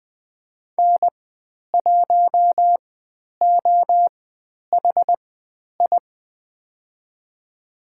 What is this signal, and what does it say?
Morse code, heard as a keyed tone, about 20 wpm, spaced as for 10 wpm, 700 Hz. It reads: N1OHI
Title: 095 Lesson 16g - Letter c - Call Signs 20wpm